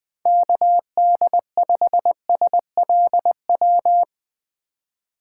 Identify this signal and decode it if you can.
Morse code, heard as a keyed tone, about 20 wpm, 700 Hz. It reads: KD5SLW
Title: USA Call Signs - 3x Character Spacing - 20wpm